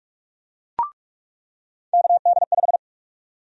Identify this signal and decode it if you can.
Morse code, heard as a keyed tone, about 46 wpm, 700 Hz. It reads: KD5
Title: USA Call Sign Prefixes - 45wpm